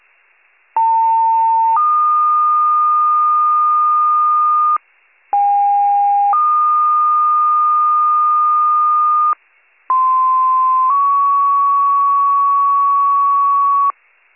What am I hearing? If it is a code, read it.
Two-tone page, heard as a one-second tone, then a three-second tone, about 900 then 1220 Hz, about 810 then 1220 Hz, about 1040 then 1120 Hz. It is heard from 0.8 to 13.9 s.